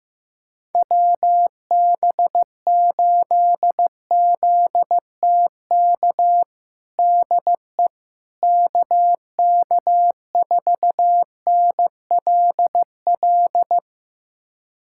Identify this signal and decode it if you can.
Morse code, heard as a keyed tone, about 15 wpm, 700 Hz. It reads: WB8ZTK DE KK4NLL
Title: USA Callsign DE Callsign - 6 Characters - 15wpm